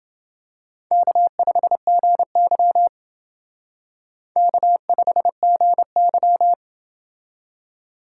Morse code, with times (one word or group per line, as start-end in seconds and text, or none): 0.9-2.9 K5GY
4.4-6.5 K5GY